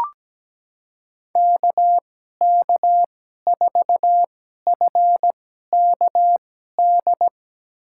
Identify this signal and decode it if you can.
Morse code, heard as a keyed tone, about 17 wpm, spaced as for 12 wpm, 700 Hz. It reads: KK4FKD
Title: USA Call Signs - 6 Characters - 2x Character Spacing - 17wpm